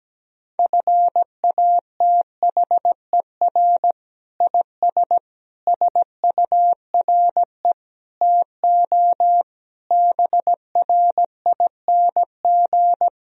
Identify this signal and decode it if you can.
Morse code, heard as a keyed tone, about 17 wpm, 700 Hz. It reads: FATHER IS SURE TO BRING
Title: Sentences from Top 1000 Words - Review Part 3 - 17wpm